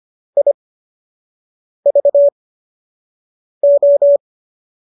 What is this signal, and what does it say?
Morse code, heard as a keyed tone, about 25 wpm, spaced as for 6 wpm, 570 Hz. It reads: IVO
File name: qso012 06